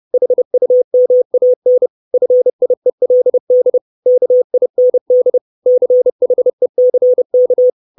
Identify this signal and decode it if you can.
Morse code, heard as a keyed tone, about 30 wpm, 500 Hz. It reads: HUMAN FIELD KIND CHECK
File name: 632 Words 30WPM